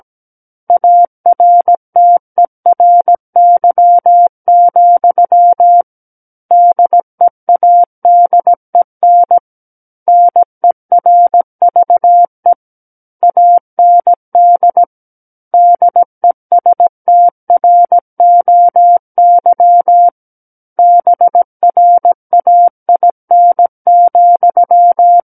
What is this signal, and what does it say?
Morse code, heard as a keyed tone, about 17 wpm, 700 Hz. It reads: ARTERY, DEADEN NERVE AND DESTROY BRAIN,
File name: Book One / 08-Chapter 8 - 17 WPM